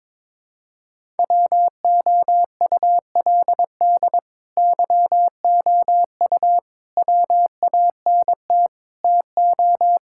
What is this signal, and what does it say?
Morse code, heard as a keyed tone, about 22 wpm, 700 Hz. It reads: WOULD YOU WANT TO
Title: Sentences from Top 1600 Words 22wpm